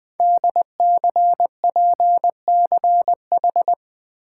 Morse code, heard as a keyed tone, about 20 wpm, 700 Hz. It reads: DCPCH
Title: Code Groups - 5 Letters - 20wpm